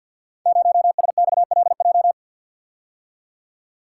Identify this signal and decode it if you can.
Morse code, heard as a keyed tone, about 48 wpm, 700 Hz. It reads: 0SXLJ
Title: Code Groups - 5 Letter:Numbers - 50wpm